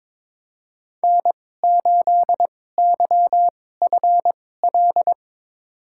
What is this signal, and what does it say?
Morse code, heard as a keyed tone, about 21 wpm, spaced as for 16 wpm, 700 Hz. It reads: N8YFL
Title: USA Call Signs - 2x Character Spacing - 22wpm